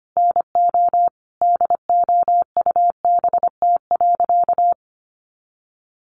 Morse code, heard as a keyed tone, about 25 wpm, 700 Hz. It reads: NO DOUBT.
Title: Sentences from Top 1500 Words 25wpm